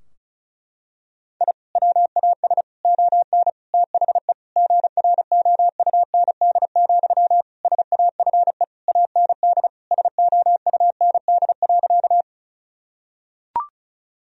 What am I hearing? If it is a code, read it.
Morse code, heard as a keyed tone, about 36 wpm, 700 Hz. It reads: I WAS ON THE GROUND, SAFE AND SOUND.
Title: Sentences from Top 1000 Words - Review Part 2 - 35wpm